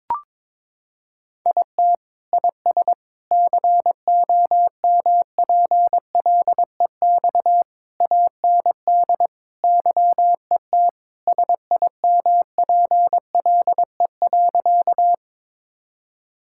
Morse code, heard as a keyed tone, about 22 wpm, 700 Hz. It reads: IT IS COMPLEX AND YET SIMPLE.
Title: Sentences from Top 1000 Words - Review Part 6 - 22wpm